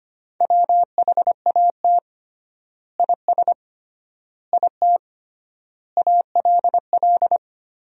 Morse code, heard as a keyed tone, about 25 wpm, 700 Hz. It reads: WHAT IS IT ALL
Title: 3x Word-Spacing - Sentences from Top 100 Words - 25wpm